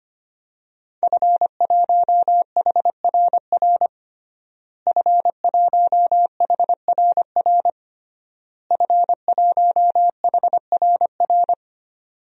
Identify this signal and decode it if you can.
Morse code, heard as a keyed tone, about 25 wpm, 700 Hz. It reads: F1HRR F1HRR F1HRR
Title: DXCC-100 Call Signs - 5 Characters - Repeated 3x - 25wpm